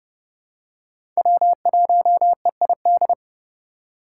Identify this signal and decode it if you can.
Morse code, heard as a keyed tone, about 30 wpm, 700 Hz. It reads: W1EID
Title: USA Call Signs - 5 Characters - 2x Character Spacing - 30wpm